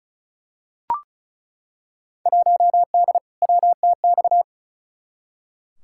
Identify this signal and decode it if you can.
Morse code, heard as a keyed tone, about 34 wpm, 700 Hz. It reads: 1D WTX
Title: Field Day Exchanges - 35wpm